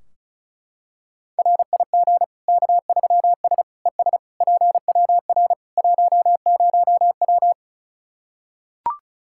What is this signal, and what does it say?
Morse code, heard as a keyed tone, about 34 wpm, 700 Hz. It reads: RIG K3S ES PWR 10W
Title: Practice QSO - Exchange 2 - Info - 35wpm